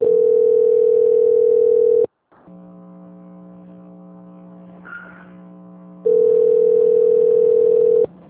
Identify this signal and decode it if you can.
Ringback tone (North America), 440 + 480 Hz.